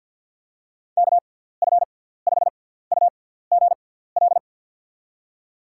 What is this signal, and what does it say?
Morse code, heard as a keyed tone, about 48 wpm, spaced as for 17 wpm, 700 Hz. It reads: KF5UGL
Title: USA Call Signs - 6 Characters - 6x Character Spacing - 50wpm